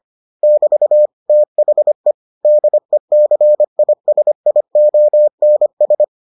Morse code, heard as a keyed tone, about 25 wpm, 600 Hz. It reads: <BT> THE DECISIONS